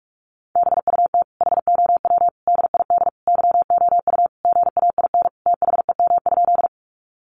Morse code, heard as a keyed tone, about 44 wpm, 700 Hz. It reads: BUT HOW DID YOU GAIN THEM?
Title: Sentences from Top 1000 Words 45wpm